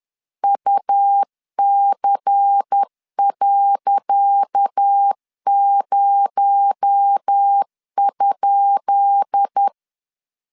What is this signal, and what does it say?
Morse code, heard as a keyed tone, about 10 wpm, 800 Hz. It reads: UC.0?
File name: LEZIONI / 10 SIGNS / 10C DOT 10wpm 2940